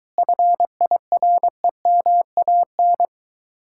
Morse code, heard as a keyed tone, about 23 wpm, 700 Hz. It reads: FIREMAN